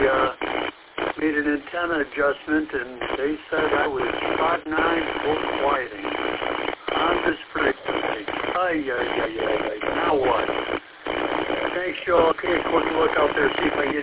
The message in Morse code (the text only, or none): none